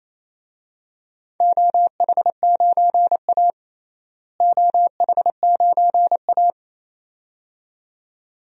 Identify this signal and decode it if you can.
Morse code, heard as a keyed tone, about 27 wpm, 700 Hz. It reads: OH9A OH9A
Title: DXCC-100 Call Signs - 4 Characters - Repeated 2x - 28wpm